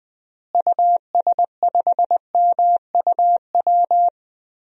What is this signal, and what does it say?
Morse code, heard as a keyed tone, about 19 wpm, 700 Hz. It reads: US5MUW
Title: DXCC-100 Call Signs - 6 Characters - Repeated 3x - 20wpm